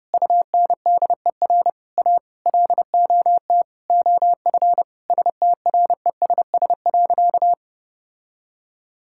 Morse code, heard as a keyed tone, about 30 wpm, 700 Hz. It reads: UNDER A LOT OF STRESS.
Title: Sentences from Top 1000 Words - Review Part 6 - 30wpm